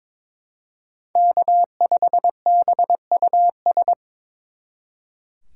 Morse code, heard as a keyed tone, about 26 wpm, 700 Hz. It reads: K5BUS
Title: Speed Racing - USA 1x3 Call Signs 22wpm